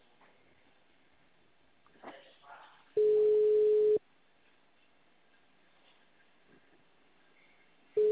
Ringback tone (Europe, 425 Hz).
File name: out- -2039- - - .1984